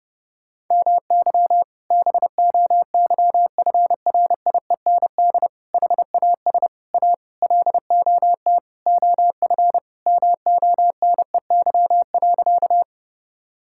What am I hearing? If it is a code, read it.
Morse code, heard as a keyed tone, about 30 wpm, 700 Hz. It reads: MY BOYFRIEND HAS A LOT OF MONEY.
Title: Sentences from Top 1000 Words - Review Part 5 - 30wpm